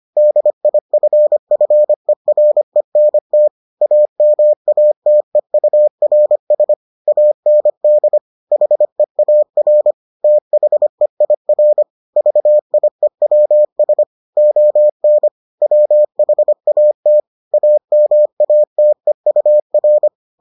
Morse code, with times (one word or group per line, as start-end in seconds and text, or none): 0.2-3.5 DIFFERENT
3.8-6.7 AMATEURS
7.1-8.2 AND
8.5-9.9 HEAR
10.2-11.8 THEIR
12.2-14.0 VIEWS
14.4-15.3 ON
15.6-17.2 WHAT
17.5-20.1 AMATEUR